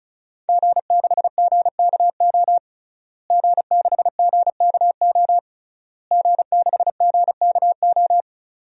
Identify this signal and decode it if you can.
Morse code, heard as a keyed tone, about 36 wpm, 700 Hz. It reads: G6GKO G6GKO G6GKO
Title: DXCC-100 Call Signs - 5 Characters - Repeated 3x - 35wpm